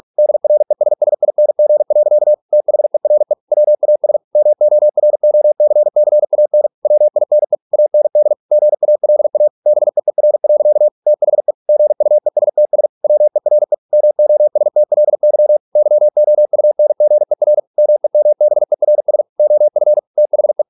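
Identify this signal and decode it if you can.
Morse code, heard as a keyed tone, about 46 wpm, 600 Hz. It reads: DRESSING. THERE WAS MOROCCAN WINE AND GALA BEER. THE GUESTS WERE MOSTLY YOUNGER MEMBERS OF THE
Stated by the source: Morse practice at 45 wpm, the novel Never by Ken Follett in English, from CW med Gnister.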